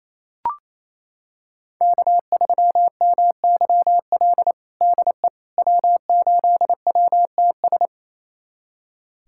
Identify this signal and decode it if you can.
Morse code, heard as a keyed tone, about 28 wpm, 700 Hz. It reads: K3MYL DE W8WTS
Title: USA Callsign DE Callsign - 5 Characters - 28wpm